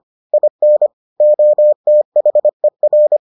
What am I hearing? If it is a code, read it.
Morse code, heard as a keyed tone, about 25 wpm, 600 Hz. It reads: IN OTHER